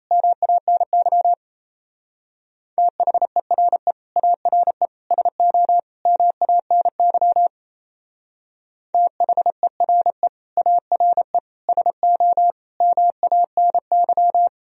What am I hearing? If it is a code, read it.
Morse code, heard as a keyed tone, about 36 wpm, 700 Hz. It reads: MANY THERE ARE SO MANY THERE ARE SO MANY